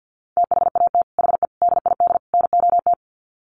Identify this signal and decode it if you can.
Morse code, heard as a keyed tone, about 48 wpm, 700 Hz. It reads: THAT HE DID NOT
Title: Sets of 4 Words 50wpm